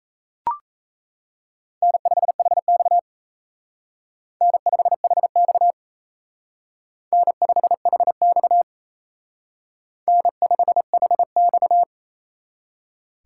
Morse code, heard as a keyed tone, about 34 wpm, 700 Hz. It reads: N5HX N5HX N5HX N5HX
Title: Speed Racing - USA Call Signs 28wpm